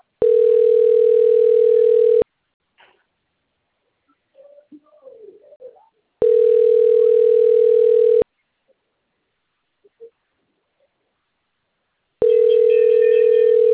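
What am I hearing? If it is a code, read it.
Ringback tone (North America), 440 + 480 Hz.
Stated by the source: Outbound phone call recorded on a PBX labelled Limeira V.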